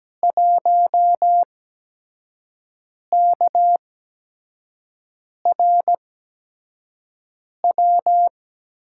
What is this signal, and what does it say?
Morse code, heard as a keyed tone, about 17 wpm, spaced as for 5 wpm, 700 Hz. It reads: 1KRW
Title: USA Call Signs - 5 Characters - 8x Character Spacing - 17wpm